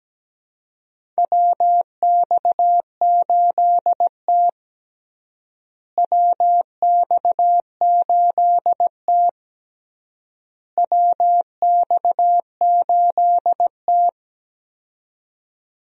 Morse code, heard as a keyed tone, about 17 wpm, 700 Hz. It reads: WX8T WX8T WX8T
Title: USA Call Signs - 4 Characters - Repeated 3x - 17wpm